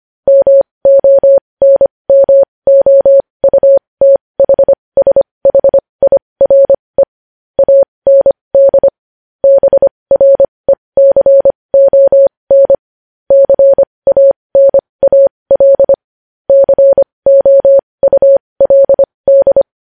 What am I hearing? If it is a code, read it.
Morse code, heard as a keyed tone, about 25 wpm, spaced as for 20 wpm, 550 Hz. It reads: MONMOUTHSHIRE AND BRECON CANAL COULD
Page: news-headlines-in-morse-code-at-20-wpm